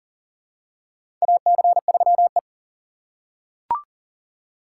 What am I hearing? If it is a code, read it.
Morse code, heard as a keyed tone, about 37 wpm, 700 Hz. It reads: AC3E
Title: USA Call Signs - 4 Characters - 2x Character Spacing - 40wpm